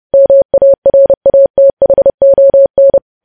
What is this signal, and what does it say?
Morse code, heard as a keyed tone, about 30 wpm, 550 Hz. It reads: MARATHON
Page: news headlines in morse code at 30 wpm